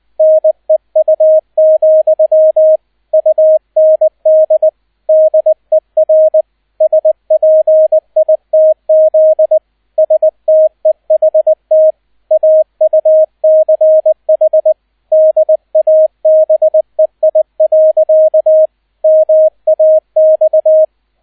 Morse code, heard as a keyed tone, about 20 wpm, 630 Hz, with radio noise. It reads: NEU, UND DER SPITZ STEHT AUCH DABEI. MAX
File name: 002 Zweiter Streich 20wpm